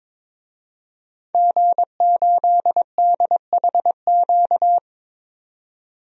Morse code, heard as a keyed tone, about 22 wpm, 700 Hz. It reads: G8DHQ
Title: DXCC-100 Call Signs - 5 Characters - 22wpm